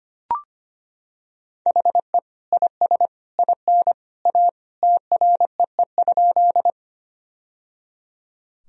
Morse code, heard as a keyed tone, about 25 wpm, 700 Hz. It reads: HE IS IN A TREE?
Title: Sentences from Top 1000 Words - Review Part 1 - 25wpm